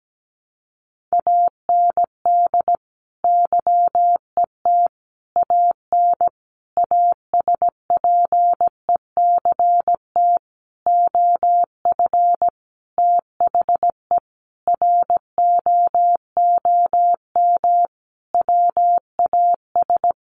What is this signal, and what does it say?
Morse code, heard as a keyed tone, about 17 wpm, 700 Hz. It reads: AND YET AN ASPECT OF THE ROOM WAS
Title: Sentences from Top 1300 Words 17wpm